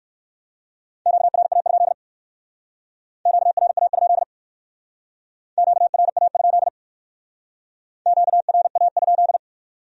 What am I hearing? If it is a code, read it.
Morse code, heard as a keyed tone, about 56 wpm, 700 Hz. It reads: QRA? QRA? QRA? QRA?
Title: Speed-Racing Q-Codes - 45wpm